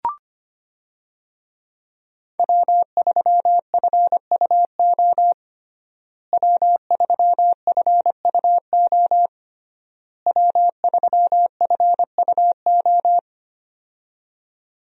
Morse code, heard as a keyed tone, about 25 wpm, 700 Hz. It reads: W3FUO W3FUO W3FUO
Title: USA Call Signs - 5 Characters - Repeated 3x - 25wpm